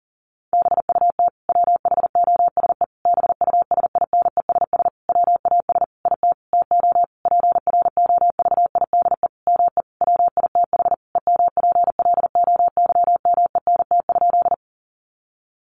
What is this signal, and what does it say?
Morse code, heard as a keyed tone, about 40 wpm, 700 Hz. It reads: BUT WHOSE BUSINESS WAS IT TO PROVIDE ME WITH EMPLOYMENT?
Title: Sentences from Top 1500 Words 40wpm